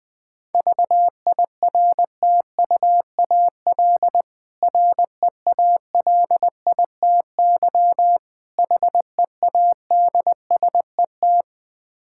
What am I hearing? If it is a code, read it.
Morse code, heard as a keyed tone, about 20 wpm, 690 Hz. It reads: VIRTUAL REALITY HEADSET
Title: Difficult Words with Hints - 20wpm